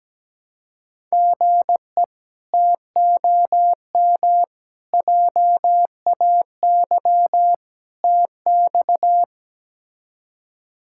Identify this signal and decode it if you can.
Morse code, heard as a keyed tone, about 17 wpm, 700 Hz. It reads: GE TOM JAY TX
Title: K1USN SST QSO Practice - 17wpm